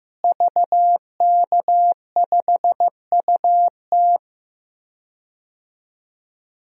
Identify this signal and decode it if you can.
Morse code, heard as a keyed tone, about 15 wpm, 700 Hz. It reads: VK5UT
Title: DXCC-100 Call Signs - 5 Characters - Repeated 2x - 15wpm